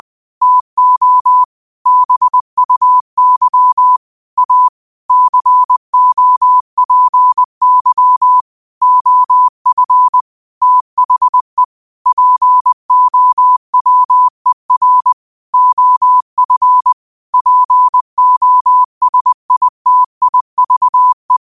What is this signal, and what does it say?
Morse code, heard as a keyed tone, about 20 wpm, 1000 Hz. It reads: TO BUY A COPY OF THE POWER OF POSITIVE
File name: qotdfiles / 20wpm